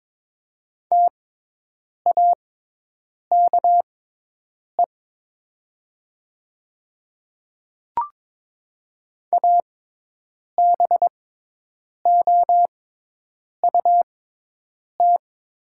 Morse code, heard as a keyed tone, about 22 wpm, spaced as for 8 wpm, 700 Hz. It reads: TAKE ABOUT
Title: 6x Character Spacing - Top 200 Words - 22wpm